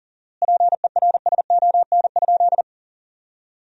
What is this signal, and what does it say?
Morse code, heard as a keyed tone, about 38 wpm, 700 Hz. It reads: PERSON?
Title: 4x Word-Spacing - Sentences from Top 400 Words - 40wpm